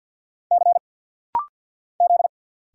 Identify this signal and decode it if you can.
Morse code, heard as a keyed tone, about 50 wpm, 700 Hz. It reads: C Z
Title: Single Letter-Number Alt - Rapid-Fire 50wpm